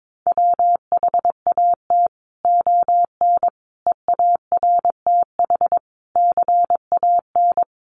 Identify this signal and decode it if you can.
Morse code, heard as a keyed tone, about 22 wpm, 700 Hz. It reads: WHAT ON EARTH CAN
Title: Sentences from Top 700 Words 22wpm